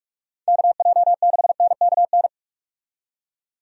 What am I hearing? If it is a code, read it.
Morse code, heard as a keyed tone, about 46 wpm, 700 Hz. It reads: KJ6NKN
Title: USA Call Signs - 6 Characters - 45wpm